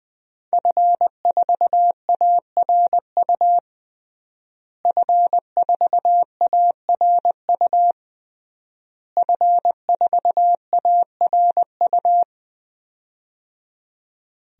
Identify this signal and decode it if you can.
Morse code, heard as a keyed tone, about 21 wpm, 700 Hz. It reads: F4ARU F4ARU F4ARU
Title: DXCC-100 Call Signs - 5 Characters - Repeated 3x - 20wpm